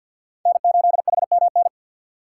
Morse code, heard as a keyed tone, about 53 wpm, 700 Hz. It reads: N8HMN
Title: USA Call Signs - 5 Characters - 4x Character Spacing - 50wpm